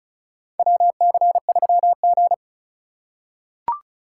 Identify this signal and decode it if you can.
Morse code, heard as a keyed tone, about 34 wpm, 700 Hz. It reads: WC3G
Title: USA Call Signs - 4 Characters - 4x Character Spacing - 35wpm